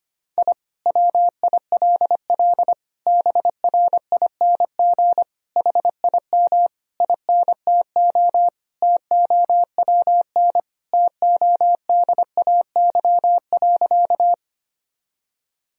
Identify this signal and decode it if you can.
Morse code, heard as a keyed tone, about 25 wpm, 700 Hz. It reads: I WILL BRING HIM INTO TOWN TODAY.
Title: Sentences from Top 1000 Words - Review Part 3 - 25wpm